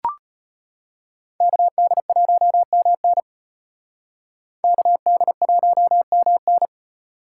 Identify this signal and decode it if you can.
Morse code, heard as a keyed tone, about 32 wpm, 700 Hz. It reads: KD1MN KD1MN